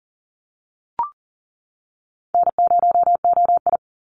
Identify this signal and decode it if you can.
Morse code, heard as a keyed tone, about 41 wpm, 700 Hz. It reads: N0OI
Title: NA SOTA Activator Callsigns 40wpm